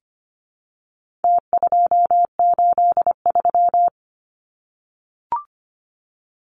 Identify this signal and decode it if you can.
Morse code, heard as a keyed tone, about 25 wpm, 700 Hz. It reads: T283
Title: Numbers - Contest Style - 4-Digit 25wpm